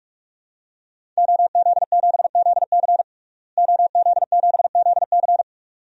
Morse code, heard as a keyed tone, about 46 wpm, 700 Hz. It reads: OZ7ZC OZ7ZC
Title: DXCC-100 Call Signs - 5 Characters - Repeated 2x - 45wpm